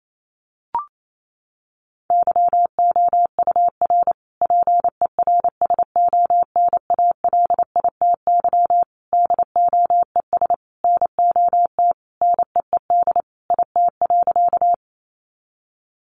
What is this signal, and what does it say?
Morse code, heard as a keyed tone, about 28 wpm, 700 Hz. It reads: YOUR PERSONALITY DOES NOT NEED IT.
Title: Sentences from Top 1400 Words 28wpm